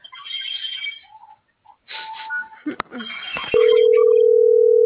Ringback tone (North America), 440 + 480 Hz.